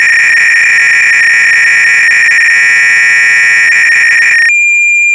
300-baud modem data.
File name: Repository / ACR-Audio-Cassette-Programs / MITS-BASIC-GAME-BATTLE-CLOADC